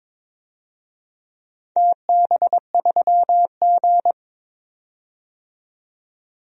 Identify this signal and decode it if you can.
Morse code, heard as a keyed tone, about 21 wpm, 700 Hz. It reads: TB3G
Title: DXCC-100 Call Signs - 4 Characters - 22wpm